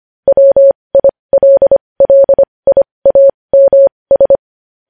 Morse code, heard as a keyed tone, about 25 wpm, spaced as for 20 wpm, 550 Hz. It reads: WILLIAMS